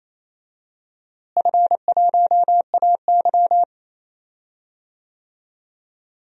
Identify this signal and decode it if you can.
Morse code, heard as a keyed tone, about 28 wpm, 700 Hz. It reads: F1AY
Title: DXCC-100 Call Signs - 4 Characters - Repeated 3x - 28wpm